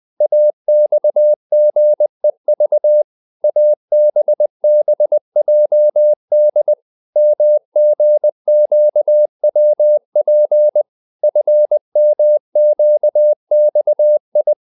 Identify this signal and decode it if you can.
Morse code, heard as a keyed tone, about 20 wpm, 600 Hz. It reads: AXGEV ABBJD MGQWP FMQXI